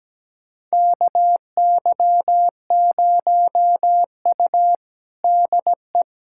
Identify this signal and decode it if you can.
Morse code, heard as a keyed tone, about 17 wpm, 700 Hz. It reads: KY0U DE
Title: USA Callsign DE Callsign - 4 Characters - 17wpm